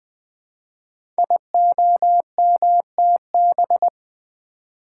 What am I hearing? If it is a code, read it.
Morse code, heard as a keyed tone, about 20 wpm, 700 Hz. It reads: IOMTB
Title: Code Groups - 5 Letters - 20wpm